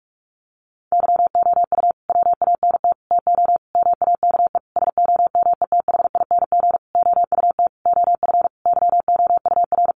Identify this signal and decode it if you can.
Morse code, heard as a keyed tone, about 44 wpm, 700 Hz. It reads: YOU WANT TO MAKE SOMETHING OUT OF YOUR
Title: Sentences from Top 600 Words 45wpm